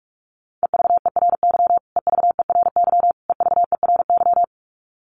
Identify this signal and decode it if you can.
Morse code, heard as a keyed tone, about 44 wpm, 700 Hz. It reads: EVERY EVERY EVERY
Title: Top 200 Words Repeated 45wpm